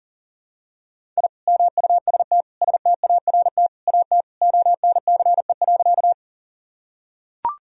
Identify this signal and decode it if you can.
Morse code, heard as a keyed tone, about 40 wpm, 700 Hz. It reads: I MUST START AT ONCE.
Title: Sentences from Top 1000 Words - Review Part 1 - 40wpm